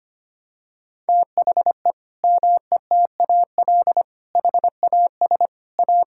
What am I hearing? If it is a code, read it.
Morse code, heard as a keyed tone, about 25 wpm, 700 Hz. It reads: THE METAL HAS A
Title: Sentences from Top 1000 Words - Review Part 6 - 25wpm